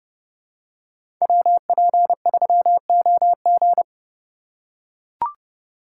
Morse code, heard as a keyed tone, about 30 wpm, 700 Hz. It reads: WP3OG
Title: USA Call Signs - 5 Characters - 2x Character Spacing - 30wpm